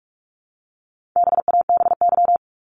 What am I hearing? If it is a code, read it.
Morse code, heard as a keyed tone, about 44 wpm, 700 Hz. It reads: BABY
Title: Top 1001-1250 Words 45wpm